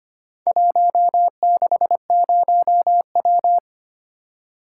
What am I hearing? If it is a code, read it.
Morse code, heard as a keyed tone, about 25 wpm, 700 Hz. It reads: 160W
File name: practice / 25wpm / Four Letter-Number Combinations - 25wpm